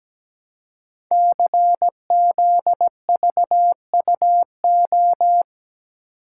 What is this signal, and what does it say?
Morse code, heard as a keyed tone, about 17 wpm, 700 Hz. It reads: CZVUO
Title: Code Groups - 5 Letter:Numbers - 17wpm